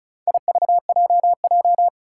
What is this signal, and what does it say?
Morse code, heard as a keyed tone, about 34 wpm, 700 Hz. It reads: IVJJ
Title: Four Challenging Letter-Number Combinations 35wpm